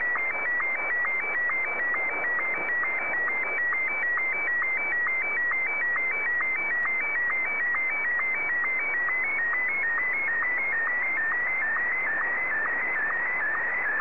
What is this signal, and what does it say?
SSTV (slow-scan television), Martin 1.
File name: sstv / M1-NOCALL